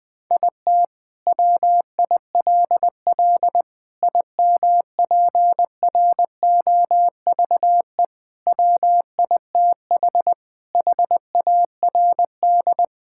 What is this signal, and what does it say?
Morse code, heard as a keyed tone, about 20 wpm, 700 Hz. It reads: IT WILL IMPROVE WITH HARD
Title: Sentences from Top 1000 Words - Review Part 5 - 20wpm